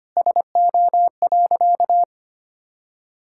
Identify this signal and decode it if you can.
Morse code, heard as a keyed tone, about 25 wpm, 700 Hz. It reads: SO.
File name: practice / 25wpm / Sentences with 3-Word Phrases - 25wpm